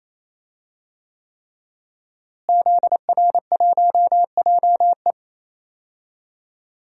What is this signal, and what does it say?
Morse code, heard as a keyed tone, about 28 wpm, 700 Hz. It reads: ZR1JE
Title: DXCC-100 Call Signs - 5 Characters - Repeated 3x - 28wpm